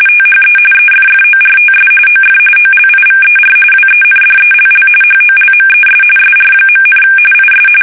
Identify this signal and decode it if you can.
RTTY, about 45.45 baud. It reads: THE EFFORTS OF OLIVER WENDELL HOLMES SR., IS
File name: fsk850 / usnavy1